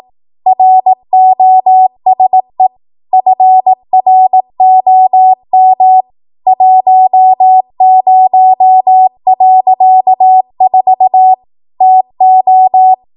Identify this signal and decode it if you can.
Morse code, heard as a keyed tone, about 18 wpm, 750 Hz. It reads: ROSE FROM 10.4 TO
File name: Archive / 18 WPM / 18WPM